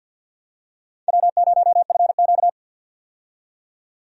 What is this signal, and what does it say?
Morse code, heard as a keyed tone, about 48 wpm, 700 Hz. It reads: W0FQ
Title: USA Call Signs - 4 Characters - 50wpm